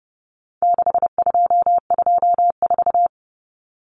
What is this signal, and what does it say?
Morse code, heard as a keyed tone, about 30 wpm, 700 Hz. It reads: 6224
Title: Common Cut Numbers - 4-Digit 30wpm